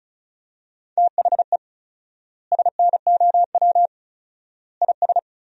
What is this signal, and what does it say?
Morse code, heard as a keyed tone, about 34 wpm, 700 Hz. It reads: THE SNOW IS
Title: 4x Word-Spacing - Sentences from Top 500 Words - 35wpm